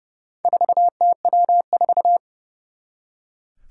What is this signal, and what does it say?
Morse code, heard as a keyed tone, about 30 wpm, 700 Hz. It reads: 4TW4
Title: Four Challenging Letter-Number Combinations 30wpm